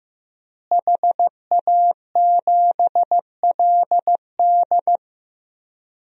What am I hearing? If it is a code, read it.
Morse code, heard as a keyed tone, about 15 wpm, 700 Hz. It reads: HA7LD
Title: DXCC-100 Call Signs - 5 Characters - Repeated 3x - 15wpm